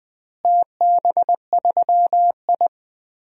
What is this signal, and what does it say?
Morse code, heard as a keyed tone, about 20 wpm, 700 Hz. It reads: TB3I
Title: DXCC-100 Call Signs - 4 Characters - Repeated 3x - 20wpm